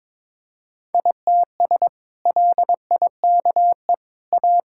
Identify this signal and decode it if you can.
Morse code, heard as a keyed tone, about 22 wpm, 700 Hz. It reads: ITS LIKE A
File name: practice / 22wpm / Sentences with 2-Word Phrases - 22wpm